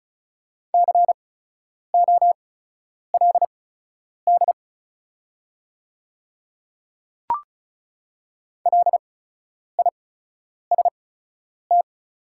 Morse code, heard as a keyed tone, about 36 wpm, spaced as for 10 wpm, 700 Hz. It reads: COLD LIST
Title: 8x Character Spacing - Top 500 Words - 35wpm